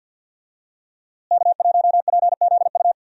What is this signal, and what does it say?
Morse code, heard as a keyed tone, about 53 wpm, 700 Hz. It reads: K1PZU
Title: USA Call Signs - 5 Characters - Repeated 2x - 50wpm